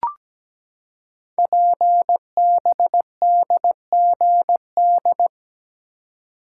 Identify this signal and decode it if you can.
Morse code, heard as a keyed tone, about 17 wpm, 700 Hz. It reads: PBDGD
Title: Code Groups - 5 Letters - 17wpm